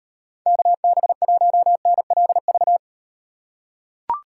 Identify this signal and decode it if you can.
Morse code, heard as a keyed tone, about 38 wpm, 700 Hz. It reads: KB1NLV